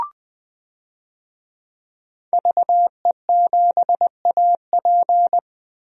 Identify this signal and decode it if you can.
Morse code, heard as a keyed tone, about 20 wpm, 700 Hz. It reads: VE7AP
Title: DXCC-100 Call Signs - 5 Characters - Repeated 3x - 20wpm